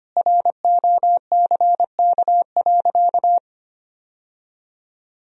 Morse code, heard as a keyed tone, about 25 wpm, 700 Hz. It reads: ROCK.
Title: Sentences from Top 1000 Words - Review Part 1 - 25wpm